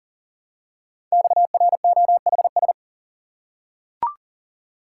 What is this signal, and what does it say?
Morse code, heard as a keyed tone, about 39 wpm, 700 Hz. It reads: XROHS
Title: Code Groups - 5 Letters - 40wpm